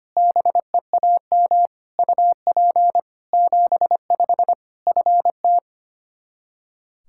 Morse code, heard as a keyed tone, about 25 wpm, 700 Hz. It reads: BEAM UP 75 FT
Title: Practice QSO - Exchange 2 - Info - 25wpm